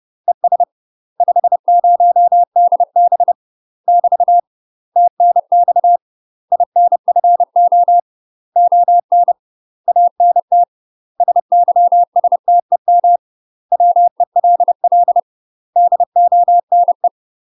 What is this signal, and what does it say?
Morse code, heard as a keyed tone, about 30 wpm, 700 Hz. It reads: ES 50DB <BT> TNX INFO ON ANT SYSTEM WELL DONE
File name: UK MASTER QSOs TRAINING / 30WPM / UK MASTER QSOs 0027